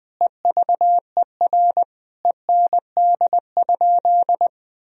Morse code, heard as a keyed tone, about 20 wpm, 700 Hz. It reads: EVER END?
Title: Sentences from Top 1000 Words - Review Part 4 - 20wpm